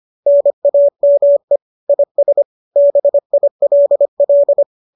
Morse code, heard as a keyed tone, about 25 wpm, 570 Hz. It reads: NAME IS BILL